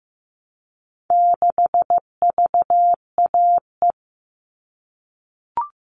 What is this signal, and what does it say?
Morse code, heard as a keyed tone, about 15 wpm, 700 Hz. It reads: 6VAE
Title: All Cut Numbers - 4-Digit 15wpm